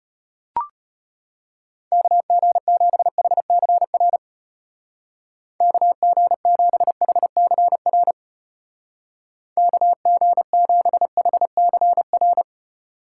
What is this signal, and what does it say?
Morse code, heard as a keyed tone, about 33 wpm, 700 Hz. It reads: KG7HCR KG7HCR KG7HCR